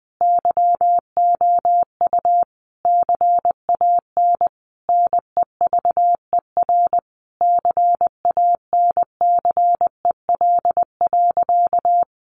Morse code, heard as a keyed tone, about 20 wpm, 700 Hz. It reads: YOU CAN NEVER CANCEL.
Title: Sentences from Top 1200 Words 20wpm